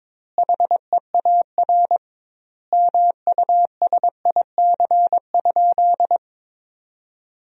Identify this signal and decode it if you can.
Morse code, heard as a keyed tone, about 22 wpm, 700 Hz. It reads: HEAR MUSIC?
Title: 2x Word-Spacing - Sentences from Top 300 Words - 22wpm